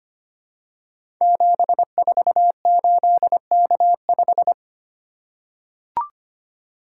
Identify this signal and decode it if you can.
Morse code, heard as a keyed tone, about 25 wpm, 700 Hz. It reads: 748K5
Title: Code Groups - 5 Letter:Numbers - 25wpm